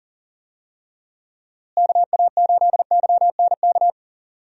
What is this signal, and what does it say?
Morse code, heard as a keyed tone, about 38 wpm, 700 Hz. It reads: KA8YNK
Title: USA Call Signs - 6 Characters - Repeated 3x - 40wpm